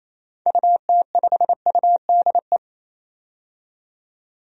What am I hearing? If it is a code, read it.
Morse code, heard as a keyed tone, about 28 wpm, 700 Hz. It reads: UT5UDE
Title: DXCC-100 Call Signs - 6 Characters - 28wpm